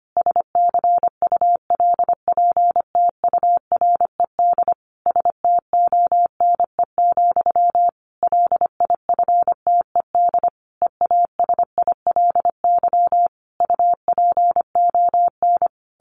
Morse code, heard as a keyed tone, about 25 wpm, 700 Hz. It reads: SCULPTURED STONE, LIFTED EASILY UPON
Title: A Sound Of Thunder 25wpm